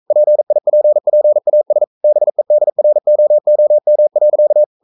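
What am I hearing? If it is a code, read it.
Morse code, heard as a keyed tone, about 42 wpm, 600 Hz. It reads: PIPPAS BEDROOM.
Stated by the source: Morse code sending of an English novel at 42 wpm.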